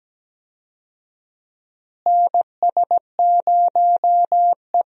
Morse code, heard as a keyed tone, about 17 wpm, 700 Hz. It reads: NS0E
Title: USA Call Signs - 4 Characters - Repeated 2x - 17wpm